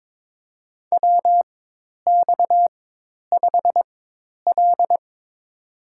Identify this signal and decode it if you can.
Morse code, heard as a keyed tone, about 21 wpm, spaced as for 10 wpm, 700 Hz. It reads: WX5L
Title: USA Call Signs - 4x Character Spacing - 22wpm